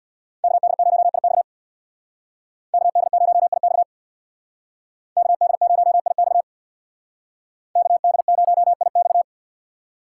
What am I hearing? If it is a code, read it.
Morse code, heard as a keyed tone, about 60 wpm, 700 Hz. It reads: KD0IX KD0IX KD0IX KD0IX
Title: Speed Racing - USA 2x2 Call Signs 50wpm